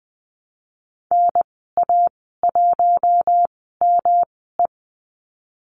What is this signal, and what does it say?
Morse code, heard as a keyed tone, about 19 wpm, spaced as for 14 wpm, 700 Hz. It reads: NA1ME
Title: 122 Lesson 20g - Number 3 - Call Signs 20wpm